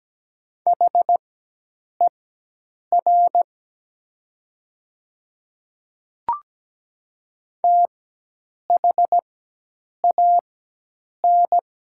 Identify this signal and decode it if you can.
Morse code, heard as a keyed tone, about 17 wpm, spaced as for 8 wpm, 700 Hz. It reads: HER THAN
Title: 4x Character Spacing - Top 100 Words - 17wpm